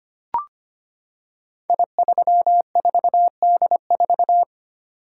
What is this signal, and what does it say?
Morse code, heard as a keyed tone, about 25 wpm, 700 Hz. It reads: I34D4
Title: Code Groups - 5 Letter:Numbers - 25wpm